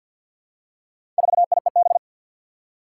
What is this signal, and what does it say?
Morse code, heard as a keyed tone, about 52 wpm, 700 Hz. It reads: 4IEB
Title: Four Challenging Letter-Number Combinations 50wpm